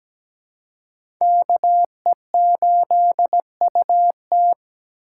Morse code, heard as a keyed tone, about 17 wpm, 700 Hz. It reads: KE8UT
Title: USA Call Signs - 5 Characters - 4x Character Spacing - 17wpm